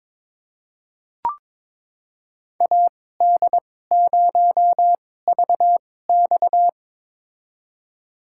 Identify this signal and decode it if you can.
Morse code, heard as a keyed tone, about 22 wpm, spaced as for 16 wpm, 700 Hz. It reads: AD0VX
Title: USA Call Signs - 5 Characters - 2x Character Spacing - 22wpm